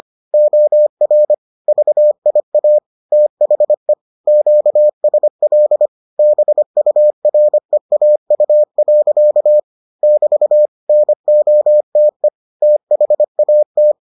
Morse code, heard as a keyed tone, about 25 wpm, 600 Hz. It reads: OR VIA THE QSL BUREAU. <BT> NOTE THAT